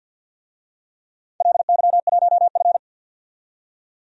Morse code, heard as a keyed tone, about 47 wpm, 700 Hz. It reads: LY1F
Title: DXCC-100 Call Signs - 4 Characters - 50wpm